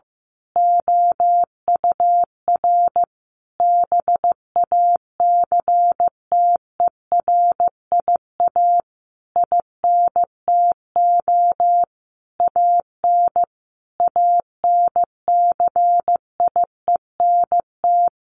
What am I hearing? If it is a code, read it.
Morse code, heard as a keyed tone, about 15 wpm, 700 Hz. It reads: OUR BACTERIA INTO AN ANCIENT
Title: A Sound Of Thunder 15wpm